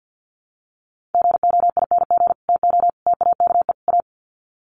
Morse code, heard as a keyed tone, about 50 wpm, 700 Hz. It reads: GOING TO TAKE A
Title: Sets of 4 Words 50wpm